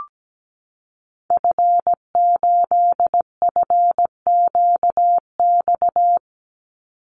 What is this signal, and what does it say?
Morse code, heard as a keyed tone, about 17 wpm, 700 Hz. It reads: F8FQX
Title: Call Signs Repeated 3x - 17wpm